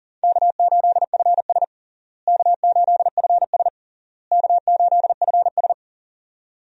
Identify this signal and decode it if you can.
Morse code, heard as a keyed tone, about 40 wpm, 700 Hz. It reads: K8FS K8FS K8FS
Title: USA Call Signs - 4 Characters - Repeated 3x - 40wpm